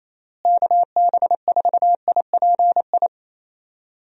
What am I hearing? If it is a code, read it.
Morse code, heard as a keyed tone, about 27 wpm, 700 Hz. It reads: KB4IPI
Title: USA Call Signs - 6 Characters - Repeated 2x - 28wpm